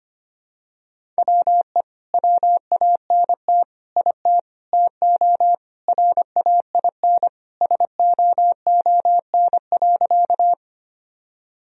Morse code, heard as a keyed tone, about 25 wpm, 700 Hz. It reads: WE WANT IT TO RAIN SOON.
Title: Sentences from Top 1000 Words - Review Part 1 - 25wpm